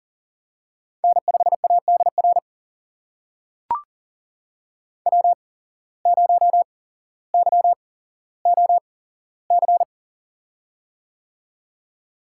Morse code, heard as a keyed tone, about 40 wpm, spaced as for 17 wpm, 700 Hz. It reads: N5ADR W0YOC
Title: USA Call Signs - 5 Characters - 8x Character Spacing - 40wpm